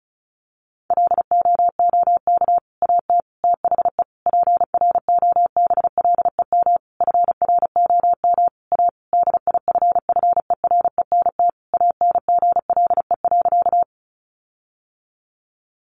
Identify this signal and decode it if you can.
Morse code, heard as a keyed tone, about 34 wpm, 700 Hz. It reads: LOOK AT THE PROBLEM FROM A DIFFERENT ANGLE.
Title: Sentences from Top 1500 Words 35wpm